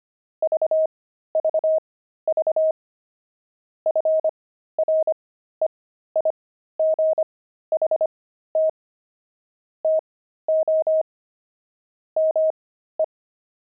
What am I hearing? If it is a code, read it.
Morse code, heard as a keyed tone, about 25 wpm, spaced as for 13 wpm, 630 Hz. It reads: VVV FREIGHT TO ME
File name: pangram-fs-13-cs-25-twelve